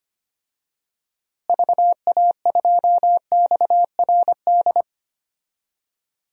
Morse code, heard as a keyed tone, about 25 wpm, 700 Hz. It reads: VA2XRD
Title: DXCC-100 Call Signs - 6 Characters - Repeated 3x - 25wpm